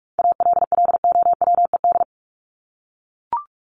Morse code, heard as a keyed tone, about 44 wpm, 700 Hz. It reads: ALLOWED
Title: ED Suffix - Top 100 Words with Base Words 45wpm